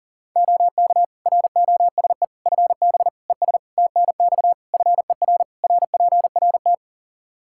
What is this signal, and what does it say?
Morse code, heard as a keyed tone, about 40 wpm, 700 Hz. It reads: OK ROSE FB ES TNX FER RPRT
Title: Practice QSO - Exchange 2 - Info - 40wpm